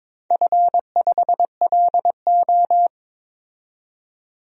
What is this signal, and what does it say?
Morse code, heard as a keyed tone, about 21 wpm, 700 Hz. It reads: F5LO
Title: DXCC-100 Call Signs - 4 Characters - 22wpm